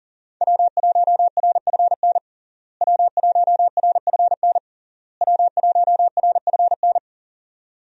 Morse code, heard as a keyed tone, about 40 wpm, 700 Hz. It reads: W1RFN W1RFN W1RFN
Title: USA Call Signs - 5 Characters - Repeated 3x - 40wpm